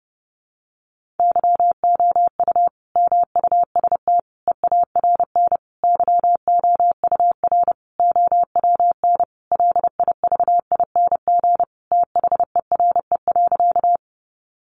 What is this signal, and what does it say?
Morse code, heard as a keyed tone, about 30 wpm, 700 Hz. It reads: YOU MUST EARN YOUR OWN LIVING THERE.
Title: Sentences from Top 1500 Words 30wpm